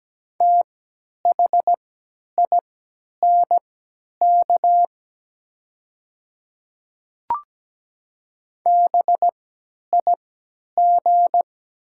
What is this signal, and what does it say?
Morse code, heard as a keyed tone, about 17 wpm, spaced as for 10 wpm, 700 Hz. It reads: THINK BIG
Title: 3x Character Spacing - Top 300 Words - 17wpm